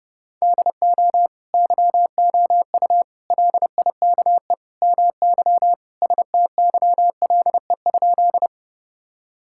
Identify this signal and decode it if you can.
Morse code, heard as a keyed tone, about 30 wpm, 700 Hz. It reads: DO YOU LIKE MY STYLE?
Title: Sentences from Top 1000 Words - Review Part 5 - 30wpm